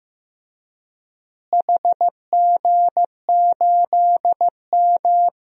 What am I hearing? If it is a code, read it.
Morse code, heard as a keyed tone, about 15 wpm, 700 Hz. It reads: HG8M
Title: DXCC-100 Call Signs - 4 Characters - Repeated 3x - 15wpm